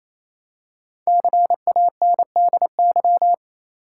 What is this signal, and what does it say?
Morse code, heard as a keyed tone, about 28 wpm, 700 Hz. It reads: CANDY
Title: Five Letter Words - 28wpm